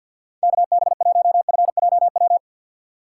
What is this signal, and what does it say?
Morse code, heard as a keyed tone, about 53 wpm, 700 Hz. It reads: KB1FJW
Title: USA Call Signs - 6 Characters - 8x Character Spacing - 50wpm